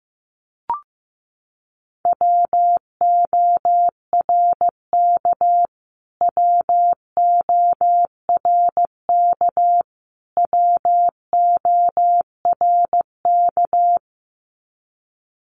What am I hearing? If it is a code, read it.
Morse code, heard as a keyed tone, about 15 wpm, 700 Hz. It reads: WORK WORK WORK
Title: Top 500 Words Repeated 15wpm